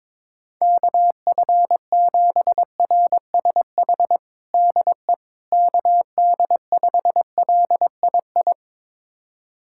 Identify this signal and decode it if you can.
Morse code, heard as a keyed tone, about 22 wpm, 700 Hz. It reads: KF7RSH DE KD5LII
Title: USA Callsign DE Callsign 22wpm